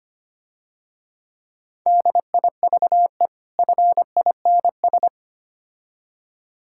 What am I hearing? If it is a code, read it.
Morse code, heard as a keyed tone, about 28 wpm, 700 Hz. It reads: DIVE FINS
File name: practice / 25wpm / Difficult Words - Speed-Racing with Hints - 25wpm